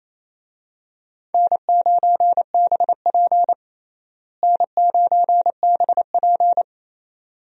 Morse code, heard as a keyed tone, about 28 wpm, 700 Hz. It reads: N9BP N9BP
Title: USA Call Signs - 4 Characters - Repeated 2x - 28wpm